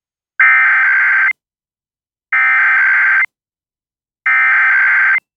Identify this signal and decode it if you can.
Emergency Alert System (SAME) tones (header bursts), heard as harsh buzzing bursts. It is heard from 0.4 to 5.2 s.